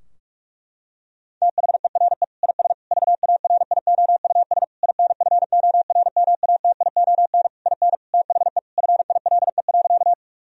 Morse code, heard as a keyed tone, about 41 wpm, 700 Hz. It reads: THERE IS VARIOUS INFORMATION IN THE FILE.
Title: Sentences from Top 1000 Words - Review Part 4 - 45wpm